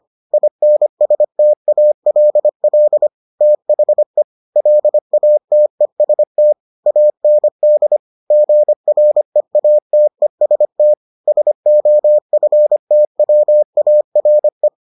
Morse code, heard as a keyed tone, about 25 wpm, 600 Hz. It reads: INSTALL THE LATEST AND GREATEST SOFTWARE